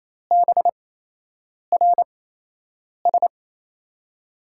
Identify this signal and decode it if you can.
Morse code, heard as a keyed tone, about 28 wpm, spaced as for 8 wpm, 700 Hz. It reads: BRS
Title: USA Call Signs - 5 Characters - 8x Character Spacing - 28wpm